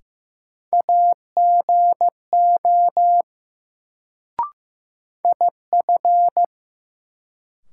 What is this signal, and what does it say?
Morse code, heard as a keyed tone, about 15 wpm, 700 Hz. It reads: AGO IF
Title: Sets of 2 Words - New Word Drill - 15wpm